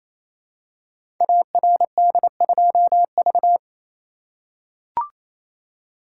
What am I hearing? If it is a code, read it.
Morse code, heard as a keyed tone, about 28 wpm, 700 Hz. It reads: ARD2V
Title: Code Groups - 5 Letter:Numbers - 28wpm